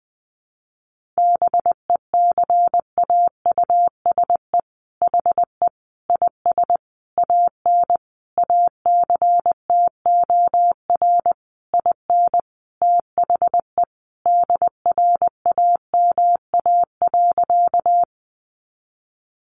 Morse code, heard as a keyed tone, about 20 wpm, 700 Hz. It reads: BECAUSE HE IS AN ACTOR IN THE DRAMA.
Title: Sentences from Top 1500 Words 20wpm